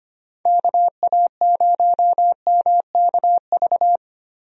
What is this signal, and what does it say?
Morse code, heard as a keyed tone, about 25 wpm, 700 Hz. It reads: KA0MKV
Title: USA Call Signs - 6 Characters - 25wpm